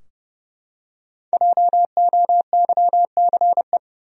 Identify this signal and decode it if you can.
Morse code, heard as a keyed tone, about 30 wpm, 700 Hz. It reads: JOYCE
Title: USA Common Names - Max 5 Letters - 30wpm